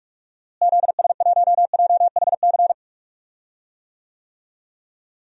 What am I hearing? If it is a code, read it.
Morse code, heard as a keyed tone, about 44 wpm, 700 Hz. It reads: ZS1JHC
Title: DXCC-100 Call Signs - 6 Characters - Repeated 3x - 45wpm